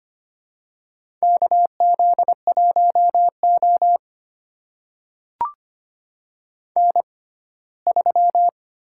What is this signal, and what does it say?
Morse code, heard as a keyed tone, about 25 wpm, spaced as for 17 wpm, 700 Hz. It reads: KZ1O N3
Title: USA Call Signs - 4 Characters - 6x Character Spacing - 25wpm